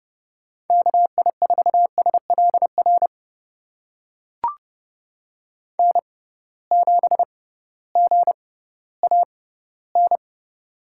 Morse code, heard as a keyed tone, about 30 wpm, spaced as for 16 wpm, 700 Hz. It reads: KI4SLR N7GAN
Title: USA Call Signs - 6x Character Spacing - 30wpm